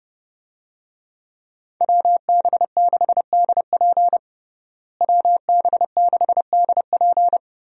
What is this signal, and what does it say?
Morse code, heard as a keyed tone, about 31 wpm, 700 Hz. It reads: WB6DP WB6DP
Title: USA Call Signs - 5 Characters - Repeated 3x - 30wpm